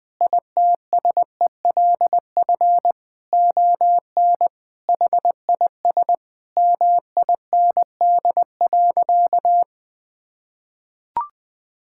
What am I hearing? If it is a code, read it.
Morse code, heard as a keyed tone, about 20 wpm, 700 Hz. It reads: ITSELF ON HIS MIND.
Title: Sentences from Top 1600 Words 20wpm